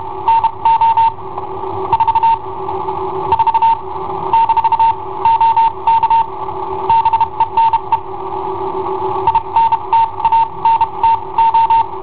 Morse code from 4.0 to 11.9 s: - OK BENE INTANTO